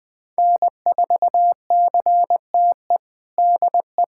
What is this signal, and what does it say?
Morse code, heard as a keyed tone, about 21 wpm, 700 Hz. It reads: N4CTE DE
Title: USA Callsign DE Callsign 20wpm